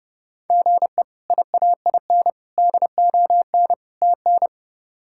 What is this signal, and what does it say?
Morse code, heard as a keyed tone, about 30 wpm, 700 Hz. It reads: GE IAIN DON TN
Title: K1USN SST QSO Practice - 30wpm